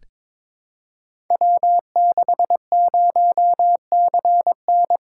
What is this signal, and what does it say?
Morse code, heard as a keyed tone, about 23 wpm, 700 Hz. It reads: W60CN
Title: Code Groups - 5 Letter:Numbers - 22wpm